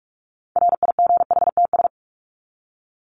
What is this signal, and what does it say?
Morse code, heard as a keyed tone, about 44 wpm, 700 Hz. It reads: RIGHTS
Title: ES:S Suffix - Top 100 Words with Base Words 45wpm